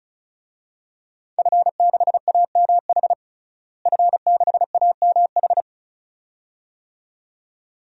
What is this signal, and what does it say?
Morse code, heard as a keyed tone, about 34 wpm, 700 Hz. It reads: F6AMH F6AMH
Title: DXCC-100 Call Signs - 5 Characters - Repeated 2x - 35wpm